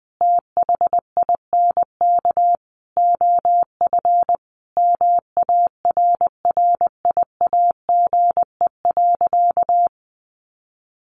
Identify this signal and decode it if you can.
Morse code, heard as a keyed tone, about 20 wpm, 700 Hz. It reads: THINK OF MARRIAGE.
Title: Sentences from Top 800 Words 20wpm